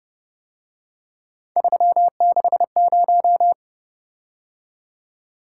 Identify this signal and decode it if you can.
Morse code, heard as a keyed tone, about 30 wpm, 700 Hz. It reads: 360
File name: Speed-Racing - Three Letter-Number Combinations - 25wpm